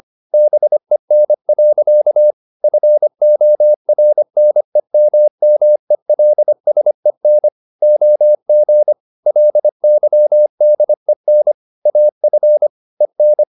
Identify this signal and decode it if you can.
Morse code, heard as a keyed tone, about 25 wpm, 600 Hz. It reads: BEN. FORNEMMELSEN OG LYDEN AF EN
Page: Ferie 25 wpm